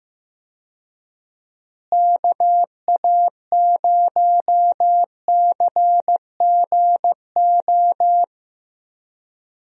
Morse code, heard as a keyed tone, about 16 wpm, 700 Hz. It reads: KA0CGO
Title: USA Call Signs - 6 Characters - Repeated 3x - 15wpm